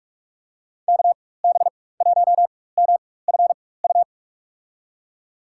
Morse code, heard as a keyed tone, about 43 wpm, spaced as for 21 wpm, 700 Hz. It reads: KB1MFU
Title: USA Call Signs - 6 Characters - 4x Character Spacing - 45wpm